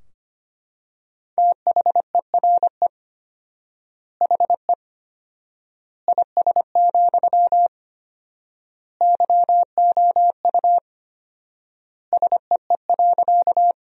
Morse code, heard as a keyed tone, about 25 wpm, 700 Hz. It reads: THERE HE IS, YOU SEE.
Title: 4x Word-Spacing - Sentences from Top 100 Words - 25wpm